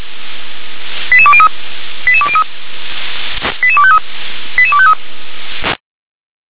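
Five-tone selective call (ZVEI), heard as quick runs of five tones, tones of about 70 ms. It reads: 88203 88203 88253 88253
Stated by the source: ZVEI five-tone selective call 88253, received on an SDR.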